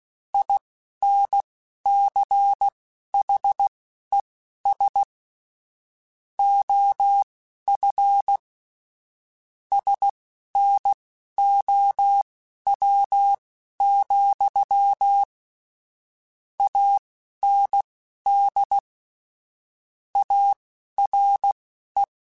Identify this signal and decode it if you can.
Morse code, heard as a keyed tone, about 16 wpm, spaced as for 11 wpm, 800 Hz. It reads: INCHES OF SNOW, AND ARE
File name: QSOs / QSOs-mp3s / WA7QOX 1@10